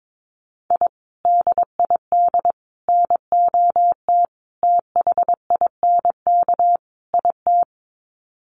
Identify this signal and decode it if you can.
Morse code, heard as a keyed tone, about 22 wpm, 700 Hz. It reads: I DID NOT THINK IT
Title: Sets of 5 Words Encore2 22wpm